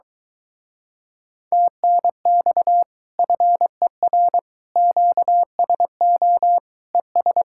Morse code, heard as a keyed tone, about 23 wpm, 700 Hz. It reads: TNX FER QSO ES